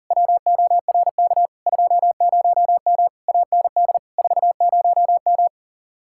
Morse code, heard as a keyed tone, about 50 wpm, 700 Hz. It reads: WORK 20M AND 40M
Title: Speed-Racing Common QSO Phrases - 40wpm